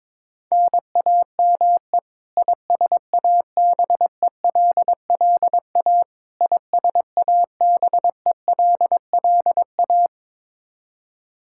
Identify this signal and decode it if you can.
Morse code, heard as a keyed tone, about 22 wpm, 700 Hz. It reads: NAME ISABELLA ISABELLA
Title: Practice QSO - Exchange 1 - Beginning - 22wpm